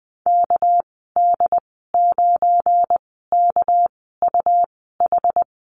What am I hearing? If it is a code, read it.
Morse code, heard as a keyed tone, about 20 wpm, spaced as for 14 wpm, 700 Hz. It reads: KD9KUH
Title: 213 Lesson 34g - Number 8 - Call Signs 20wpm